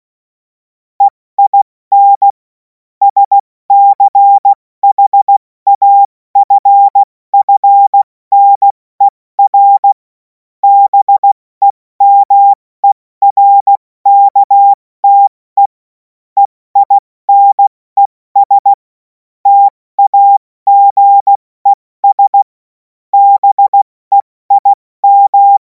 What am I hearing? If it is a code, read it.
Morse code, heard as a keyed tone, about 16 wpm, spaced as for 14 wpm, 800 Hz. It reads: EIN SCHAFFNER BEMERKTE EINES TAGES BEIM
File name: Morsekurs / Witze / Witz 03. Februar 2026 70 F80